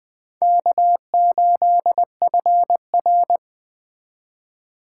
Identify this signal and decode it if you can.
Morse code, heard as a keyed tone, about 20 wpm, 700 Hz. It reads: K8FR